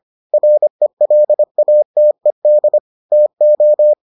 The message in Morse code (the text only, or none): RELATED TO